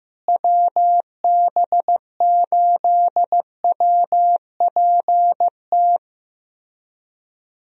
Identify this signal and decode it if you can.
Morse code, heard as a keyed tone, about 16 wpm, 700 Hz. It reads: WB8WPT
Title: USA Call Signs - 6 Characters - Repeated 3x - 15wpm